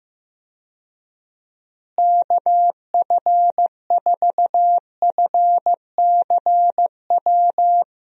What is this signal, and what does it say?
Morse code, heard as a keyed tone, about 15 wpm, 700 Hz. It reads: KF4FCW
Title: USA Call Signs - 6 Characters - Repeated 3x - 15wpm